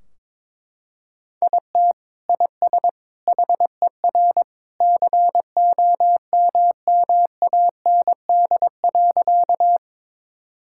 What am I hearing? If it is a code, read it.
Morse code, heard as a keyed tone, about 22 wpm, 700 Hz. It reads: IT IS HER COMMAND.
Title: Sentences from Top 1600 Words 22wpm